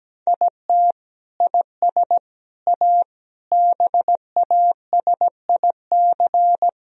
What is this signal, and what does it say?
Morse code, heard as a keyed tone, about 17 wpm, 700 Hz. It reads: IT IS A BASIC
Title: Sentences from Top 1000 Words - Review Part 3 - 17wpm